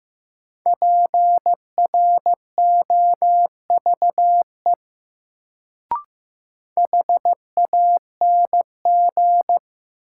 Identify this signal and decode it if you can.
Morse code, heard as a keyed tone, about 15 wpm, 700 Hz. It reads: PROVE HANG
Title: Sentences from Top 1600 Words - New Word Drill - 15wpm